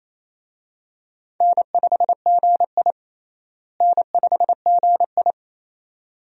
Morse code, heard as a keyed tone, about 28 wpm, 700 Hz. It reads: N5GI N5GI
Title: USA Call Signs - 4 Characters - Repeated 2x - 28wpm